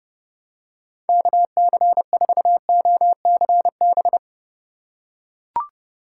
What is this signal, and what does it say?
Morse code, heard as a keyed tone, about 30 wpm, 700 Hz. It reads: KC4OCB
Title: USA Call Signs - 6 Characters - 4x Character Spacing - 30wpm